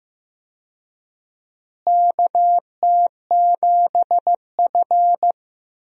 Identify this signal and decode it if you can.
Morse code, heard as a keyed tone, about 15 wpm, 700 Hz. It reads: KT7F
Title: USA Call Signs - 4 Characters - Repeated 2x - 15wpm